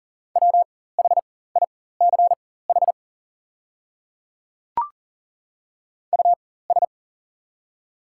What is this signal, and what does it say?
Morse code, heard as a keyed tone, about 38 wpm, spaced as for 18 wpm, 690 Hz. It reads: WHICH US
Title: 4x Character Spacing - Top 200 Words - 40wpm